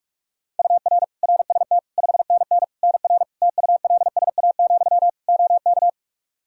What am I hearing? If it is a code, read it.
Morse code, heard as a keyed tone, about 42 wpm, 700 Hz. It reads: UR RST 5NN NR TULSA, OK
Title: Practice QSO - Exchange 1 - Beginning - 45wpm